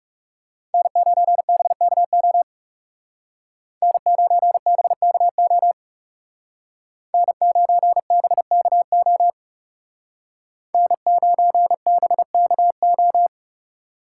Morse code, heard as a keyed tone, about 34 wpm, 700 Hz. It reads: N9BKO N9BKO N9BKO N9BKO
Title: Speed Racing - USA 1x3 Call Signs 30wpm